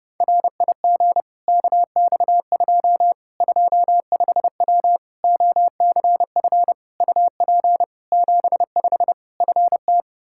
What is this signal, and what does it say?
Morse code, heard as a keyed tone, about 30 wpm, 700 Hz. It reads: RIG KX2 25W OCF UP 75 FT
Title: K1USN SST QSO Practice 30wpm